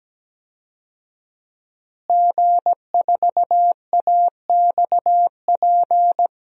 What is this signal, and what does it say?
Morse code, heard as a keyed tone, about 17 wpm, 700 Hz. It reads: G4AXP
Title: DXCC-100 Call Signs - 5 Characters - Repeated 2x - 17wpm